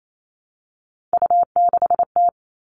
Morse code, heard as a keyed tone, about 28 wpm, 700 Hz. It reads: U6T
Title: All Cut Numbers - 3-Digit 28wpm